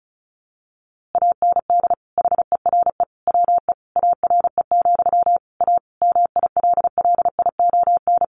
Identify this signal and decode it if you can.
Morse code, heard as a keyed tone, about 34 wpm, 700 Hz. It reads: AND HERE WE ARE, A MILLION
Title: A Sound Of Thunder 35wpm